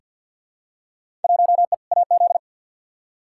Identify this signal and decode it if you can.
Morse code, heard as a keyed tone, about 49 wpm, 700 Hz. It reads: 1E AZ
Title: Field Day Exchanges - 50wpm